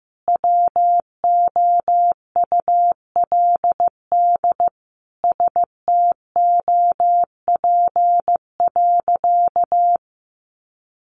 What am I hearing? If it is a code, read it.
Morse code, heard as a keyed tone, about 15 wpm, 700 Hz. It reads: WOULD STOP.
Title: Sentences from Top 1300 Words 15wpm